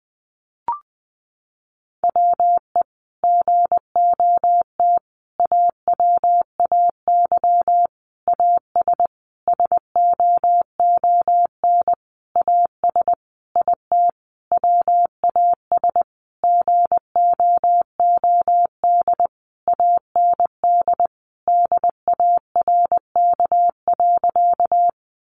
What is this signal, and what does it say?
Morse code, heard as a keyed tone, about 19 wpm, 700 Hz. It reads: WE GOT AWAY AS SOON AS IT WAS GOOD AND DARK.
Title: Sentences from Top 600 Words 20wpm